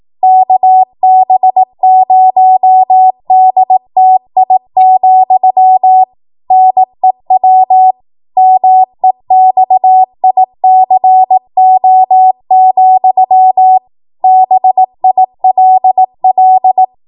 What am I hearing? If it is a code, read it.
Morse code, heard as a keyed tone, about 18 wpm, 750 Hz. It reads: KB0DTI, NEW MEXICO, BILL
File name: Morse / 18WPM